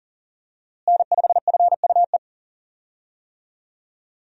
Morse code, heard as a keyed tone, about 40 wpm, 700 Hz. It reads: N5FUE
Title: USA Call Signs - 5 Characters - Repeated 2x - 40wpm